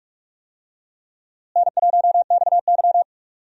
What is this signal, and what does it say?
Morse code, heard as a keyed tone, about 47 wpm, 700 Hz. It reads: N1XY
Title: USA Call Signs - 4 Characters - Repeated 3x - 45wpm